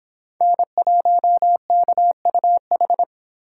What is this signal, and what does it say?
Morse code, heard as a keyed tone, about 26 wpm, 700 Hz. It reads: N1KUH